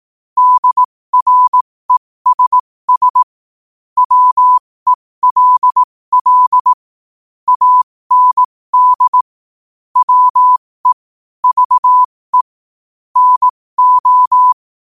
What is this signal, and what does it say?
Morse code, heard as a keyed tone, about 18 wpm, spaced as for 15 wpm, 1000 Hz. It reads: DRESS WELL AND WE VE NO
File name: qotdfiles / 15wpm